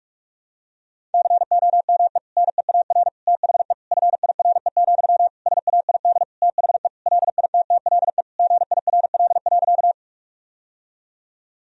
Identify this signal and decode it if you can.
Morse code, heard as a keyed tone, about 46 wpm, 700 Hz. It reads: COME NEAR THE FIRE, SAID THE LITTLE GIRL.
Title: Sentences from Top 1000 Words - Review Part 1 - 45wpm